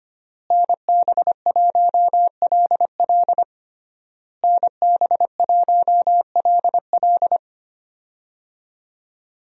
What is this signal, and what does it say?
Morse code, heard as a keyed tone, about 25 wpm, 700 Hz. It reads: NB1LL NB1LL
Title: USA Call Signs - 5 Characters - Repeated 3x - 25wpm